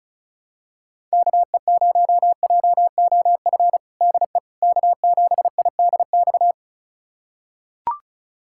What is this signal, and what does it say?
Morse code, heard as a keyed tone, about 34 wpm, 700 Hz. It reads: KE0JOF DE K7IDX
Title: USA Callsign DE Callsign 35wpm